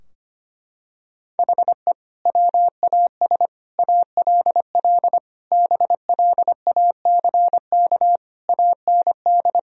Morse code, heard as a keyed tone, about 25 wpm, 700 Hz. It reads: HE WAS ALL BLACK AND
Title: Sentences from Top 1000 Words - Review Part 3 - 25wpm